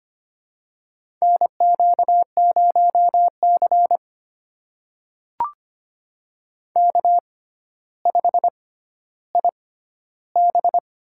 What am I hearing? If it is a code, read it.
Morse code, heard as a keyed tone, about 25 wpm, spaced as for 13 wpm, 700 Hz. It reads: NQ0C K5IB
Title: USA Call Signs - 4 Characters - 6x Character Spacing - 25wpm